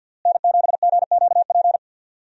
Morse code, heard as a keyed tone, about 51 wpm, 700 Hz. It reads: N7GQP
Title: USA Call Signs - 5 Characters - Repeated 3x - 50wpm